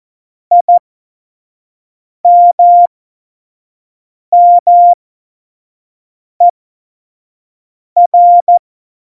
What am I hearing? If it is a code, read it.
Morse code, heard as a keyed tone, about 14 wpm, spaced as for 5 wpm, 700 Hz. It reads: IMMER